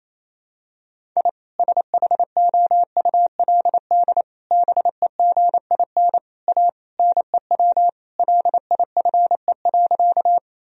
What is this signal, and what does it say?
Morse code, heard as a keyed tone, about 28 wpm, 700 Hz. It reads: I SHOULD BEGIN A NEW LIFE.
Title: Sentences from Top 1000 Words - Review Part 1 - 28wpm